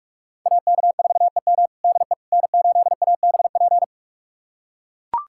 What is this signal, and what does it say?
Morse code, heard as a keyed tone, about 44 wpm, 700 Hz. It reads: AK4EM DE N8ABP
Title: USA Callsign DE Callsign - 5 Characters - 45wpm